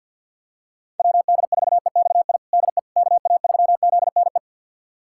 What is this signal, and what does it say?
Morse code, heard as a keyed tone, about 51 wpm, 700 Hz. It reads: WD4EXI DE KA3ZNE
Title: USA Callsign DE Callsign - 6 Characters - 50wpm